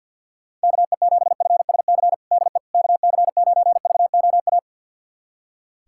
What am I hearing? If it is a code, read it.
Morse code, heard as a keyed tone, about 46 wpm, 690 Hz. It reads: KE7FSC DE KC9VOA
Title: USA Callsign DE Callsign - 6 Characters - 50wpm